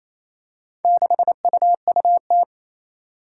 Morse code, heard as a keyed tone, about 28 wpm, 700 Hz. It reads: 6UUT
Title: Four Challenging Letter-Number Combinations 28wpm